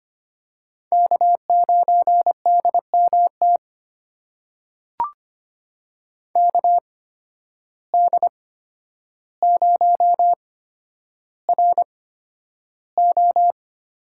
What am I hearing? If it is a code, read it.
Morse code, heard as a keyed tone, about 25 wpm, spaced as for 11 wpm, 700 Hz. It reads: K9DMT KD0RO
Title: USA Call Signs - 5 Characters - 8x Character Spacing - 25wpm